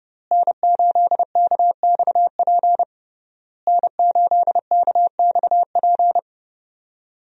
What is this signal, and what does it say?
Morse code, heard as a keyed tone, about 30 wpm, 700 Hz. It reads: N8KXP N8KXP
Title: USA Call Signs - 5 Characters - Repeated 3x - 30wpm